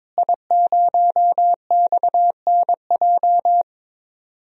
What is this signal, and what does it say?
Morse code, heard as a keyed tone, about 22 wpm, 700 Hz. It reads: I0XNJ
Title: DXCC-100 Call Signs - 5 Characters - Repeated 2x - 22wpm